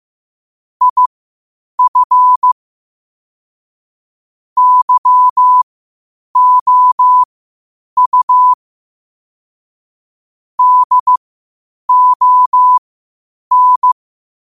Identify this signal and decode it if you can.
Morse code, heard as a keyed tone, about 15 wpm, spaced as for 8 wpm, 1000 Hz. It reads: IF YOU DON
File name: qotdfiles / 8wpm